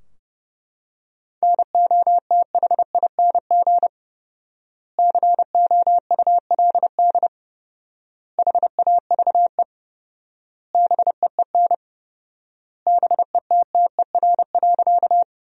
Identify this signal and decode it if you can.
Morse code, heard as a keyed tone, about 30 wpm, 700 Hz. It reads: NOTHING COULD HAVE BEEN BETTER.
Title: 4x Word-Spacing - Sentences from Top 500 Words - 30wpm